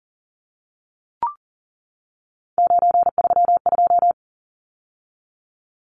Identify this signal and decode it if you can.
Morse code, heard as a keyed tone, about 40 wpm, 700 Hz. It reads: 932
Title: Numbers - 3-Digit 40wpm